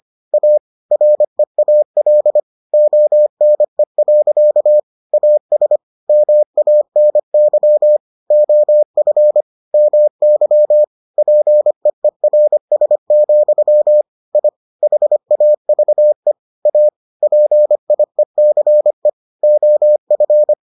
Morse code, heard as a keyed tone, about 25 wpm, 600 Hz. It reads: A REAL ONE. AS MANY OF MY PEERS, I HAVE A PIECE OF